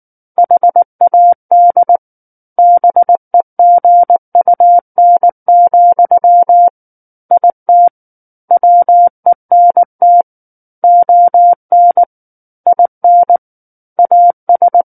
Morse code, heard as a keyed tone, about 19 wpm, 700 Hz. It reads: HAD BEGUN, IT WENT ON IN AS
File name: Book One / 10-Chapter 10 - 19 WPM